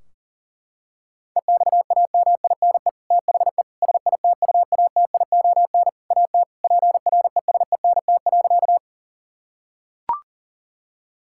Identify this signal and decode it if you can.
Morse code, heard as a keyed tone, about 40 wpm, 700 Hz. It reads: EXAMINE THE SITUATION AT PRESENT.
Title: Sentences from Top 1600 Words 40wpm